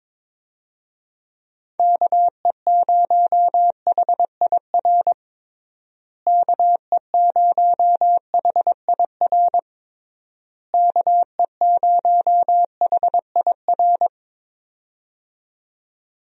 Morse code, heard as a keyed tone, about 21 wpm, 700 Hz. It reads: KE0HIR KE0HIR KE0HIR
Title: USA Call Signs - 6 Characters - Repeated 3x - 22wpm